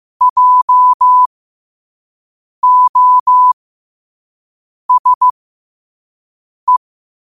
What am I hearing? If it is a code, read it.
Morse code, heard as a keyed tone, about 15 wpm, spaced as for 5 wpm, 1000 Hz. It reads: JOSE
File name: qotdfiles / 5wpm